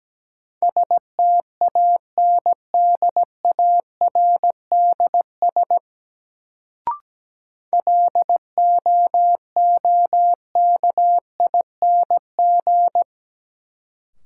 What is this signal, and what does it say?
Morse code, heard as a keyed tone, about 17 wpm, 700 Hz. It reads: STANDARDS LOOKING
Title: Sentences from Top 1000 Words - Review Part 6 - New Word Drill - 17wpm